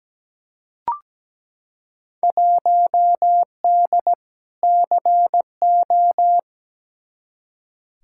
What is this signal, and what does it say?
Morse code, heard as a keyed tone, about 17 wpm, 700 Hz. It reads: 1D CO
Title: Field Day Exchanges - 17wpm